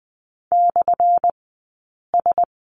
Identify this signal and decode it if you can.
Morse code, heard as a keyed tone, about 20 wpm, 700 Hz. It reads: / S
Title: 183 Lesson 30d - Letter b - Two Characters 20wpm